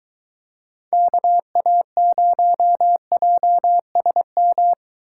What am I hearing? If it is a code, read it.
Morse code, heard as a keyed tone, about 21 wpm, 700 Hz. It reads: KA0JSM